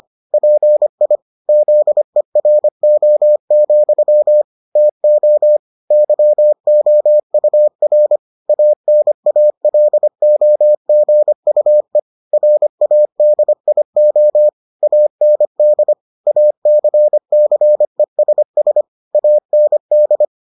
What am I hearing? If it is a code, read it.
Morse code, heard as a keyed tone, about 25 wpm, 600 Hz. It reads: PI ZERO, TO YOUR ANALOGUE RADIO AND ACCESS AND